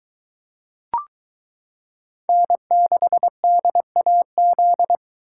Morse code, heard as a keyed tone, about 23 wpm, 700 Hz. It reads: N6DAZ